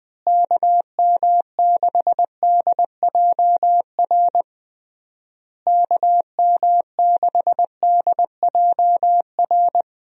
Morse code, heard as a keyed tone, about 19 wpm, 700 Hz. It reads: KM6DJR KM6DJR
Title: USA Call Signs - 6 Characters - Repeated 3x - 20wpm